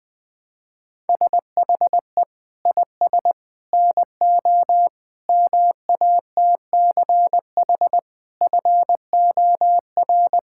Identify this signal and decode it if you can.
Morse code, heard as a keyed tone, about 20 wpm, 700 Hz. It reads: SHE IS NO MATCH FOR
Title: Sentences from Top 1600 Words 20wpm